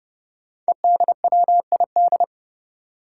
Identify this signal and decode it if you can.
Morse code, heard as a keyed tone, about 31 wpm, 700 Hz. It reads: EDWID
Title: Five Challenging Letter-Number Combinations 30wpm